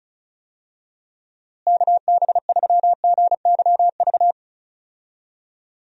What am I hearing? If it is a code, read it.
Morse code, heard as a keyed tone, about 36 wpm, 700 Hz. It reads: KB3GYV
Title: USA Call Signs - 6 Characters - Repeated 2x - 35wpm